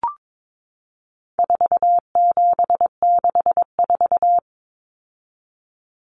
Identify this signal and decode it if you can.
Morse code, heard as a keyed tone, about 22 wpm, 700 Hz. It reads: 4764
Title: Numbers - 4-Digit 22wpm